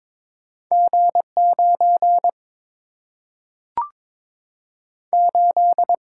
Morse code, heard as a keyed tone, about 22 wpm, 700 Hz. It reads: G9 8
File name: practice / 22wpm / Two Letter-Number Combinations - 22wpm